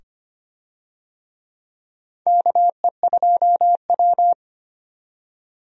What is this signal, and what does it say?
Morse code, heard as a keyed tone, about 25 wpm, 700 Hz. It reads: KE2W
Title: USA Call Signs - 4 Characters - 25wpm